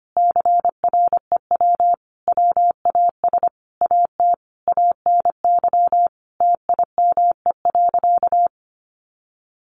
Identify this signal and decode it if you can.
Morse code, heard as a keyed tone, about 25 wpm, 700 Hz. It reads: CREW WAS AT ANY TIME.
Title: Sentences from Top 1500 Words 25wpm